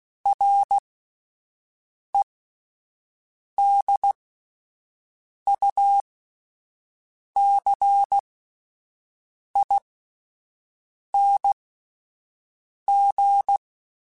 Morse code, heard as a keyed tone, about 16 wpm, spaced as for 5 wpm, 800 Hz. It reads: REDUCING
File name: QSOs / QSOs-mp3s / WD0FBT 1@10